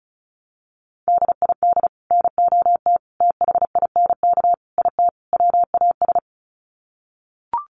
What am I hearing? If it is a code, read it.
Morse code, heard as a keyed tone, about 34 wpm, 700 Hz. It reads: DID NOT THINK IT WAS
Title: Sets of 5 Words Encore 35wpm